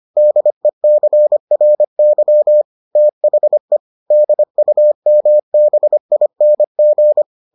Morse code, heard as a keyed tone, about 25 wpm, 600 Hz. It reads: DECRY THE DUMBING